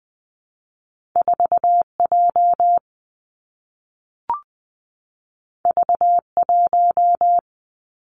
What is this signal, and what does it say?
Morse code, heard as a keyed tone, about 19 wpm, 700 Hz. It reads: 4 J V 1
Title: Two Challenging Letter-Number Combinations 20wpm